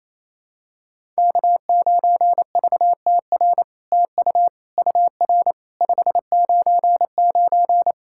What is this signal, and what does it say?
Morse code, heard as a keyed tone, about 29 wpm, 700 Hz. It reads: K9VTR TU UR 599
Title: SOTA QSO Practice 28wpm